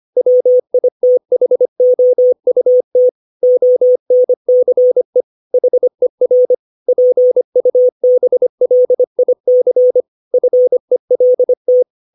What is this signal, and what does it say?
Morse code, heard as a keyed tone, about 25 wpm, 500 Hz. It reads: WITHOUT ONCE HER PUBLIC FELT
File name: 632 words / 632 Words 25WPM